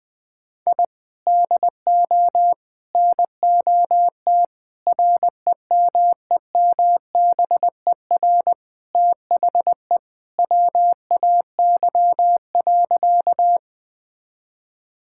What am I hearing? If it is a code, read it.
Morse code, heard as a keyed tone, about 20 wpm, 700 Hz. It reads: I DO NOT REMEMBER THE WAY.
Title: Sentences with 4-Word Phrases - Spoken Beginning - 20wpm